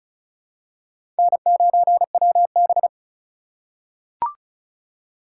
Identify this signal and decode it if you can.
Morse code, heard as a keyed tone, about 36 wpm, 700 Hz. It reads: N9WB
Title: USA Call Signs - 4 Characters - 3x Character Spacing - 35wpm